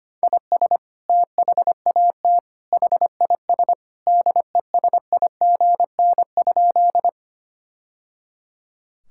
Morse code, heard as a keyed tone, about 25 wpm, 700 Hz. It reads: IS THAT HIS DESIGN?
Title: Sentences from Top 1000 Words - Review Part 5 - 25wpm